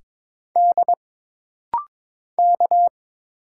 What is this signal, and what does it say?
Morse code, heard as a keyed tone, about 22 wpm, 700 Hz. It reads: D K
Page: This rapid-fire timing is geared to push your ICR (Instant Character Recognition) skills.